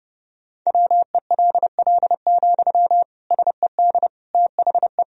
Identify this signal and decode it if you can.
Morse code, heard as a keyed tone, about 30 wpm, 700 Hz. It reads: WELL, SED THE
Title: Sentences from Top 400 Words and Abbreviations 30wpm